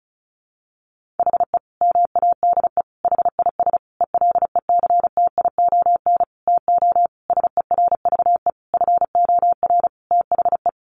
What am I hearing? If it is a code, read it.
Morse code, heard as a keyed tone, about 34 wpm, 700 Hz. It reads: HE MADE HIS ELECTION TO SERVE FOR THE
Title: Sentences from Top 1500 Words 35wpm